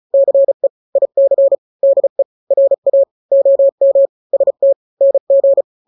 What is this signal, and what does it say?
Morse code, heard as a keyed tone, about 34 wpm, 550 Hz. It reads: CE IC DE RA OM ST NG
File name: n-grams / 2-grams 35 30 WPM